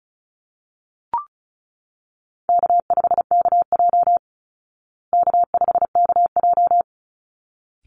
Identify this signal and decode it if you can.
Morse code, heard as a keyed tone, about 34 wpm, 700 Hz. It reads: K5KJ K5KJ
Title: Call Signs Repeated 3x - 35wpm